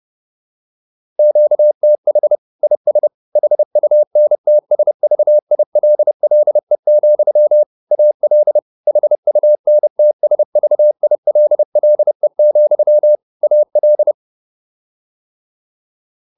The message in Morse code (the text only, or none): QTH IS HUNTSVILLE, AL HUNTSVILLE, AL